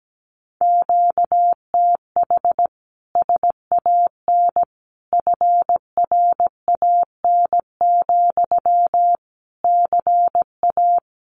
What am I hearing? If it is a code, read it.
Morse code, heard as a keyed tone, about 17 wpm, 700 Hz. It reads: QTH SAN FRAN, CA
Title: Common QSO Phrases Encore 17wpm